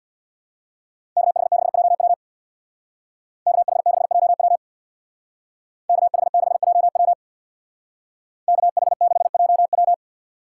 Morse code, heard as a keyed tone, about 62 wpm, 700 Hz. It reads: KH6JW KH6JW KH6JW KH6JW
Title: Speed Racing - USA 2x2 Call Signs 50wpm